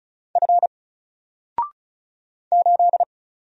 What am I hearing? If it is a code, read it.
Morse code, heard as a keyed tone, about 36 wpm, 700 Hz. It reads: F 8
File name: Single-Letter-Number-35wpm